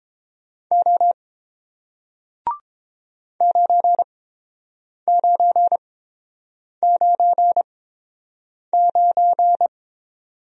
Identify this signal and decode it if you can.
Morse code, heard as a keyed tone, about 30 wpm, spaced as for 7 wpm, 700 Hz. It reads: O 9999